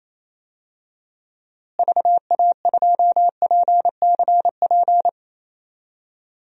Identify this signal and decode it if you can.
Morse code, heard as a keyed tone, about 28 wpm, 700 Hz. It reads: VA2PCP
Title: DXCC-100 Call Signs - 6 Characters - Repeated 3x - 28wpm